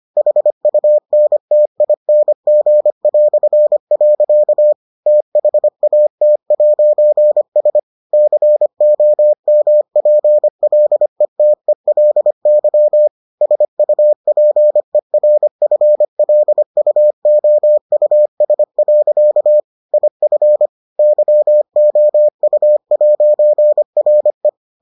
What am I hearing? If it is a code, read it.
Morse code, heard as a keyed tone, about 25 wpm, 600 Hz. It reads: HUNTING". THAT'S COMPLETELY SUPERFLUOUS. IF YOU'RE